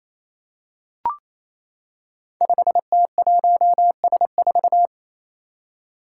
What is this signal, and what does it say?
Morse code, heard as a keyed tone, about 28 wpm, 700 Hz. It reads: 5T1S4
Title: Five Challenging Letter-Number Combinations 28wpm